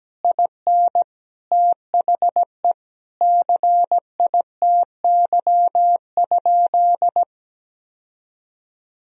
Morse code, heard as a keyed tone, about 17 wpm, 700 Hz. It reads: IN THE CITY?
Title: Sentences from Top 1000 Words - Review Part 1 - 17wpm